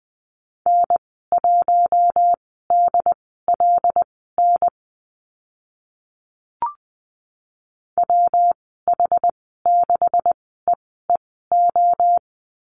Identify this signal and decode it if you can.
Morse code, heard as a keyed tone, about 20 wpm, spaced as for 14 wpm, 700 Hz. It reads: N1DLN WH6EEO
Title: 167 Lesson 27g - Number 7 - Call Signs 20wpm